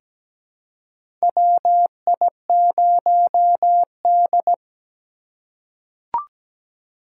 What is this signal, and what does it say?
Morse code, heard as a keyed tone, about 17 wpm, 700 Hz. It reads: WI0D
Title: USA Call Signs - 4 Characters - 2x Character Spacing - 17wpm